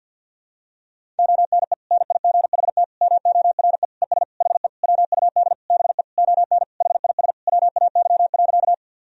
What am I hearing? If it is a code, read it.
Morse code, heard as a keyed tone, about 50 wpm, 700 Hz. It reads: ONE NIGHT MORE ES HE WUD BE ON HIS WAY.
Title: Sentences from Top 300 Words and Abbreviations 50wpm